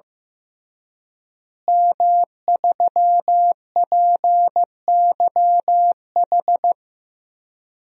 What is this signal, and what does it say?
Morse code, heard as a keyed tone, about 15 wpm, 700 Hz. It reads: M3PYH
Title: DXCC-100 Call Signs - 5 Characters - Repeated 3x - 15wpm